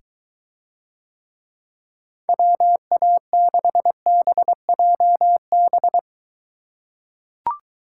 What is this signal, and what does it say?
Morse code, heard as a keyed tone, about 22 wpm, 700 Hz. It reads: WA6BJB